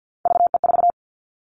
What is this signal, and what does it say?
Morse code, heard as a keyed tone, about 50 wpm, 700 Hz. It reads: VE4
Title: Callsign Prefixes for Top 25 DXCC - 50wpm